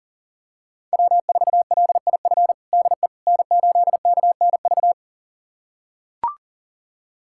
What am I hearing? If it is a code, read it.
Morse code, heard as a keyed tone, about 40 wpm, 700 Hz. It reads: W4LIF DE N8KNV
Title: USA Callsign DE Callsign - 5 Characters - 40wpm